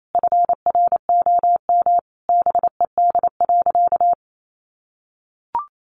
Morse code, heard as a keyed tone, about 28 wpm, 700 Hz. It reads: FROM BED.
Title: Sentences from Top 900 Words 28wpm